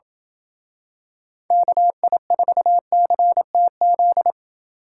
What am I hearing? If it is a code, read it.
Morse code, heard as a keyed tone, about 28 wpm, 700 Hz. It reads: KI4CTZ